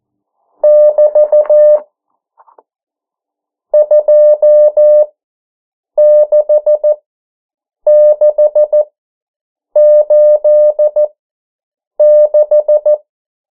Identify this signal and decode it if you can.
Morse code, heard as a keyed tone, about 14 wpm, spaced as for 7 wpm, 600 Hz. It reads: <BT> 26686
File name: morse mp3 / 09 urok 8